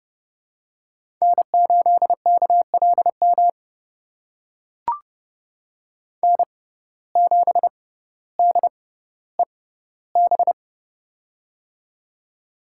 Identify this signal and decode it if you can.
Morse code, heard as a keyed tone, about 30 wpm, spaced as for 15 wpm, 700 Hz. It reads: N8KLM N7DEB
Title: USA Call Signs - 5 Characters - 6x Character Spacing - 30wpm